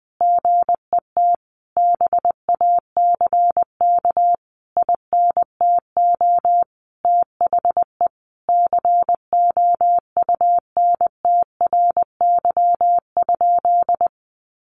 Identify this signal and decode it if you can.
Morse code, heard as a keyed tone, about 20 wpm, 700 Hz. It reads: GET BACK INTO THE COUNTRY?
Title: Sentences from Top 1400 Words 20wpm